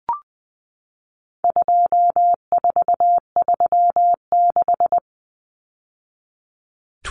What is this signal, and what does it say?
Morse code, heard as a keyed tone, about 20 wpm, 700 Hz. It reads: 2436
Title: Numbers - 4-Digit 20wpm